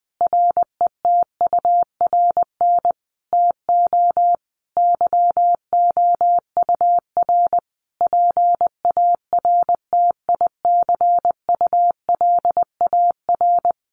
Morse code, heard as a keyed tone, about 20 wpm, 700 Hz. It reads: RETURN TO YOUR PARTICULAR
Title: Sentences from Top 800 Words 20wpm